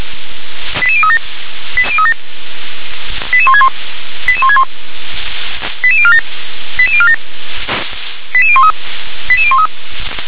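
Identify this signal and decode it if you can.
Five-tone selective call (ZVEI), heard as quick runs of five tones, tones of about 70 ms. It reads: 88037 88161 88161 88047 88047 88013 88013
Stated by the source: ZVEI five-tone call 88187 received by SDR.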